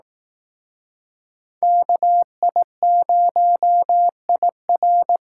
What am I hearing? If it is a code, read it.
Morse code, heard as a keyed tone, about 19 wpm, 700 Hz. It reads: KI0IR